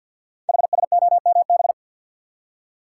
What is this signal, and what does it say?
Morse code, heard as a keyed tone, about 48 wpm, 690 Hz. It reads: HSOMB
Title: Code Groups - 5 Letters - 50wpm